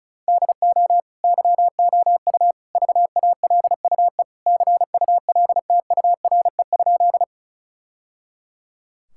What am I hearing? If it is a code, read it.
Morse code, heard as a keyed tone, about 36 wpm, 700 Hz. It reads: DO YOU VALUE CULTURE?
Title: Sentences from Top 1000 Words - Review Part 6 - 35wpm